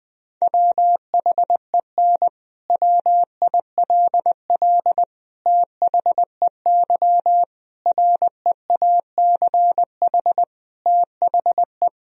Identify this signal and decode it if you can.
Morse code, heard as a keyed tone, about 20 wpm, 700 Hz. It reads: WHEN WILL THEY REACH THE
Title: Sentences from Top 1000 Words - Review Part 4 - 20wpm